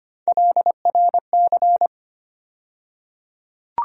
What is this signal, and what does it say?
Morse code, heard as a keyed tone, about 25 wpm, 700 Hz. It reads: LRC
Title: USA Call Sign Postfixes - 3 Letters 25wpm